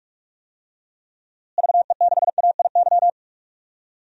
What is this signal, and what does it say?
Morse code, heard as a keyed tone, about 46 wpm, 700 Hz. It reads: VE6AIY
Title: DXCC-100 Call Signs - 6 Characters - Repeated 3x - 45wpm